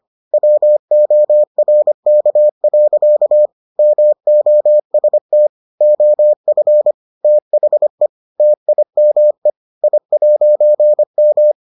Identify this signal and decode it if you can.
Morse code, heard as a keyed tone, about 25 wpm, 600 Hz. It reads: WORK. MOST OF THE TIME I'M